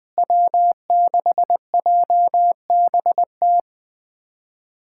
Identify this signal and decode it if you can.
Morse code, heard as a keyed tone, about 20 wpm, 700 Hz. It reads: W6JBT
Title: USA Callsign DE Callsign - 5 Characters - 20wpm